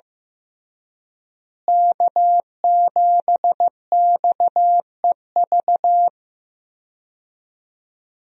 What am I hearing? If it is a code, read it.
Morse code, heard as a keyed tone, about 15 wpm, 700 Hz. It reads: K7XEV
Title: USA Call Signs - 5 Characters - Repeated 2x - 15wpm